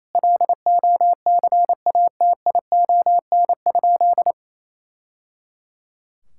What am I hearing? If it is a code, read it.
Morse code, heard as a keyed tone, about 28 wpm, 700 Hz. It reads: LOCATION?
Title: Sentences from Top 1000 Words - Review Part 6 - 28wpm